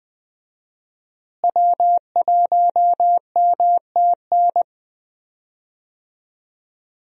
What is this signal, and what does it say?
Morse code, heard as a keyed tone, about 20 wpm, 700 Hz. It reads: W1MTN
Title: USA Call Signs - 5 Characters - Repeated 3x - 20wpm